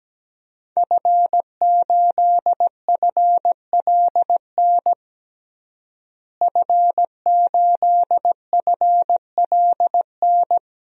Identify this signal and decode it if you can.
Morse code, heard as a keyed tone, about 17 wpm, 700 Hz. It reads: F8FLN F8FLN
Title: DXCC-100 Call Signs - 5 Characters - Repeated 2x - 17wpm